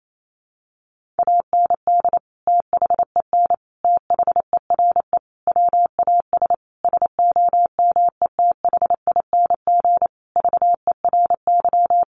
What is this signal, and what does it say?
Morse code, heard as a keyed tone, about 28 wpm, 700 Hz. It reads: AND THEN THERE WAS SOMETHING VERY
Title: Sentences from Top 1400 Words 28wpm